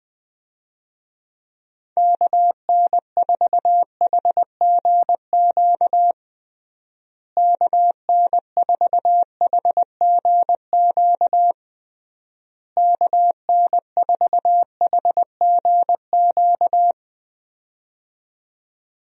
Morse code, heard as a keyed tone, about 21 wpm, 700 Hz. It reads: KN4HGQ KN4HGQ KN4HGQ
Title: USA Call Signs - 6 Characters - Repeated 3x - 20wpm